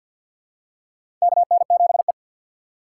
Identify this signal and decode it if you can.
Morse code, heard as a keyed tone, about 53 wpm, 700 Hz. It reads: KN7E
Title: USA Call Signs - 4 Characters - 3x Character Spacing - 50wpm